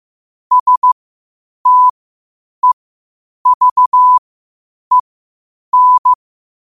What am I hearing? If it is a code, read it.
Morse code, heard as a keyed tone, about 15 wpm, spaced as for 8 wpm, 1000 Hz. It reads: STEVEN